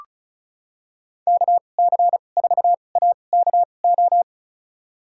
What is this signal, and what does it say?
Morse code, heard as a keyed tone, about 34 wpm, spaced as for 25 wpm, 690 Hz. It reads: KC4AKO
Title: USA Call Signs - 6 Characters - 2x Character Spacing - 35wpm